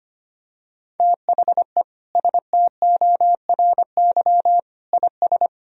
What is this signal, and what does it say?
Morse code, heard as a keyed tone, about 25 wpm, 700 Hz. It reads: THE STORY IS
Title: Sentences from Top 1000 Words - Review Part 4 - 25wpm